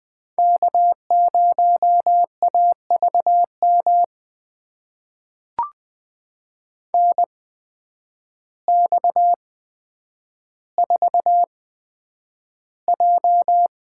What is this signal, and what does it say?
Morse code, heard as a keyed tone, about 20 wpm, spaced as for 9 wpm, 700 Hz. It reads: K0AVM NX4J
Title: USA Call Signs - 8x Character Spacing - 20wpm